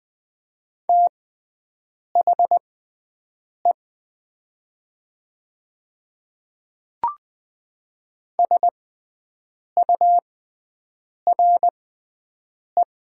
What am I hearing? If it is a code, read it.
Morse code, heard as a keyed tone, about 20 wpm, spaced as for 7 wpm, 700 Hz. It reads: THE SURE
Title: 6x Character Spacing - Top 300 Words - 20wpm